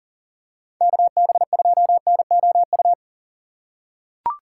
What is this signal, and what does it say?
Morse code, heard as a keyed tone, about 40 wpm, 700 Hz. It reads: KB2NOU
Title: USA Call Signs - 6 Characters - 2x Character Spacing - 40wpm